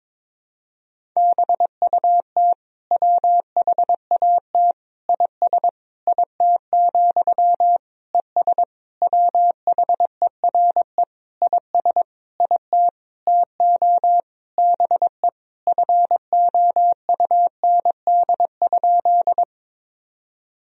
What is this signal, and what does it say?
Morse code, heard as a keyed tone, about 22 wpm, 700 Hz. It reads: BUT WHAT IS IT, ES WHERE IS IT TO BE FOUND?
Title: Sentences from Top 300 Words and Abbreviations 22wpm